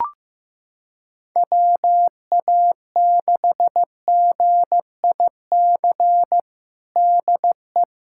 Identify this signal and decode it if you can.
Morse code, heard as a keyed tone, about 15 wpm, 700 Hz. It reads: WA6GIC DE
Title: USA Callsign DE Callsign 15wpm